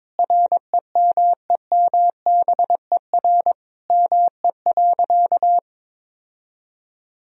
Morse code, heard as a keyed tone, about 22 wpm, 700 Hz. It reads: REMEMBER ME.
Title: Sentences from Top 1000 Words - Review Part 1 - 22wpm